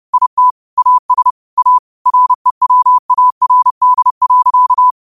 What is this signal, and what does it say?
Morse code, heard as a keyed tone, about 30 wpm, 1000 Hz. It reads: IT AS A REWARD.